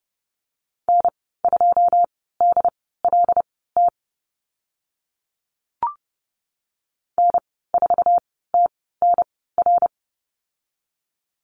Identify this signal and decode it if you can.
Morse code, heard as a keyed tone, about 30 wpm, spaced as for 17 wpm, 700 Hz. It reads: N2DLT N4TNR
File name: Lesson 15g - Number 5 - Call Signs 30wpm